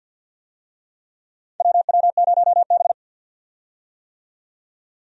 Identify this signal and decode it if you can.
Morse code, heard as a keyed tone, about 53 wpm, 700 Hz. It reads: WW0B
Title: USA Call Signs - 4 Characters - 50wpm